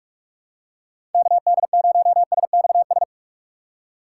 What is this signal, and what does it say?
Morse code, heard as a keyed tone, about 44 wpm, 700 Hz. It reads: KD0SXS
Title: USA Call Signs - 6 Characters - 6x Character Spacing - 45wpm